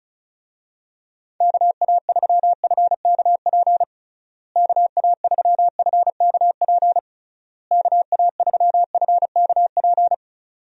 Morse code, heard as a keyed tone, about 34 wpm, 700 Hz. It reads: KA3FKP KA3FKP KA3FKP
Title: USA Call Signs - 6 Characters - Repeated 3x - 35wpm